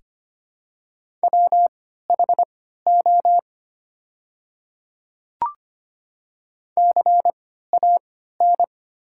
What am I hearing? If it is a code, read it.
Morse code, heard as a keyed tone, about 25 wpm, spaced as for 14 wpm, 700 Hz. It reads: WHO CAN
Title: 3x Character Spacing - Top 100 Words - 25wpm